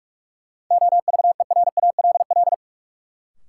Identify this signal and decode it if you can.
Morse code, heard as a keyed tone, about 44 wpm, 700 Hz. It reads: OVERALL
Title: Sentences from Top 1000 Words - Review Part 7 - New Word Drill - 45wpm